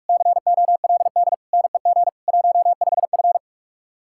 Morse code, heard as a keyed tone, about 44 wpm, 700 Hz. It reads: COLD NEG 15F